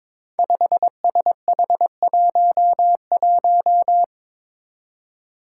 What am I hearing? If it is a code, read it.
Morse code, heard as a keyed tone, about 22 wpm, 700 Hz. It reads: 5SH11
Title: Five Challenging Letter-Number Combinations 22wpm